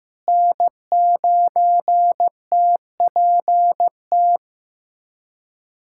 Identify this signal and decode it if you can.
Morse code, heard as a keyed tone, about 15 wpm, 700 Hz. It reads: N9TPT
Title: USA Callsign DE Callsign 15wpm